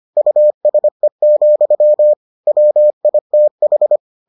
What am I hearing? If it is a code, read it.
Morse code, heard as a keyed tone, about 25 wpm, 600 Hz. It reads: USE, WITH